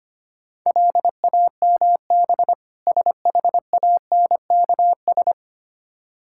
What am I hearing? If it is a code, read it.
Morse code, heard as a keyed tone, about 25 wpm, 700 Hz. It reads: LAMB SHANKS
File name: practice / 25wpm / Difficult Words with Hints - 25wpm